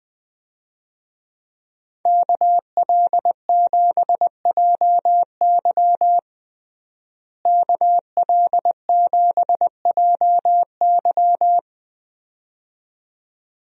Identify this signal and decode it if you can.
Morse code, heard as a keyed tone, about 20 wpm, 700 Hz. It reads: KL7JY KL7JY
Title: DXCC-100 Call Signs - 5 Characters - Repeated 2x - 20wpm